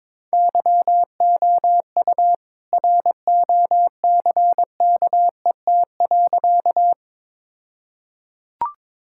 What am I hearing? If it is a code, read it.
Morse code, heard as a keyed tone, about 22 wpm, 700 Hz. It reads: YOU ROCKET.
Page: morse-code-ninja-practice-22wpm